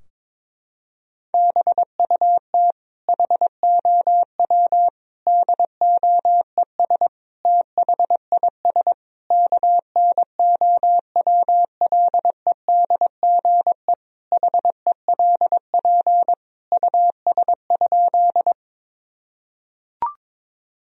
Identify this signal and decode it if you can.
Morse code, heard as a keyed tone, about 21 wpm, 700 Hz. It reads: BUT HOW DOES THIS KNOWLEDGE HELP US?
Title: Sentences from Top 1000 Words - Review Part 3 - 22wpm